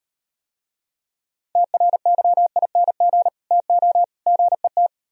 Morse code, heard as a keyed tone, about 38 wpm, 700 Hz. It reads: TRYING TO GET